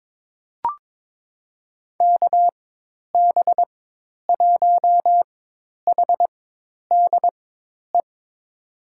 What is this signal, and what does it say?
Morse code, heard as a keyed tone, about 22 wpm, spaced as for 10 wpm, 700 Hz. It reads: KB1HDE
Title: USA Call Signs - 6 Characters - 4x Character Spacing - 22wpm